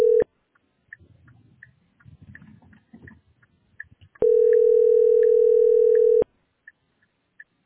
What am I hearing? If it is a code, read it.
Ringback tone (North America), 440 + 480 Hz.
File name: 9886cd74-ee19-4819-9dff